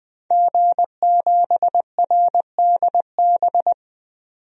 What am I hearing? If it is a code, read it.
Morse code, heard as a keyed tone, about 21 wpm, 700 Hz. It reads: G7RDB
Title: DXCC-100 Call Signs - 5 Characters - 20wpm